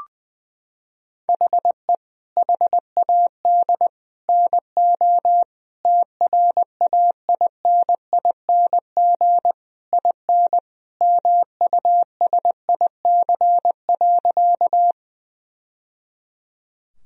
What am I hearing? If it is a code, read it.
Morse code, heard as a keyed tone, about 20 wpm, 700 Hz. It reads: HE HAD NO TRAINING IN MUSIC.
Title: Sentences from Top 1000 Words - Review Part 4 - 20wpm